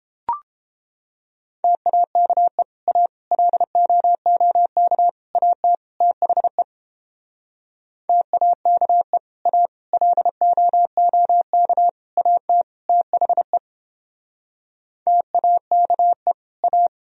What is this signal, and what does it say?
Morse code, heard as a keyed tone, about 29 wpm, 700 Hz. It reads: TAKE A LOOK AT THE TAKE A LOOK AT THE TAKE A